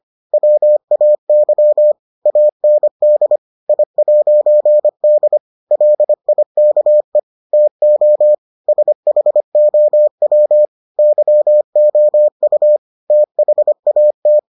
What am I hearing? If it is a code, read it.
Morse code, heard as a keyed tone, about 25 wpm, 600 Hz. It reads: WAY AND I'D LIKE TO SHOW YOU THAT